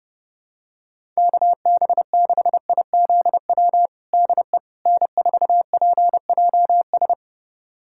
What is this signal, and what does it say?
Morse code, heard as a keyed tone, about 32 wpm, 700 Hz. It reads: KB6IZW DE N4PJS
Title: USA Callsign DE Callsign 30wpm